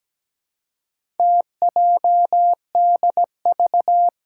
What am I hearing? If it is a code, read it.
Morse code, heard as a keyed tone, about 17 wpm, 700 Hz. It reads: TJDV
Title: Four Challenging Letter-Number Combinations 17wpm